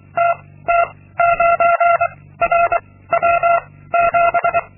Morse code, heard as a keyed tone, about 23 wpm, 690 Hz. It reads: T T 9 R W 7
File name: Linksmakalnis-2